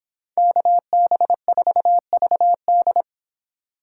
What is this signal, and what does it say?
Morse code, heard as a keyed tone, about 29 wpm, 700 Hz. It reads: KB4VD